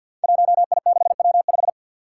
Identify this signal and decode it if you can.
Morse code, heard as a keyed tone, about 53 wpm, 700 Hz. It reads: 1I6W5
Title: Five Challenging Letter-Number Combinations 50wpm